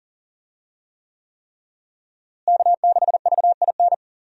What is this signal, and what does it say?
Morse code, heard as a keyed tone, about 40 wpm, 700 Hz. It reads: K6VIN
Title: USA Call Signs - 5 Characters - Repeated 3x - 40wpm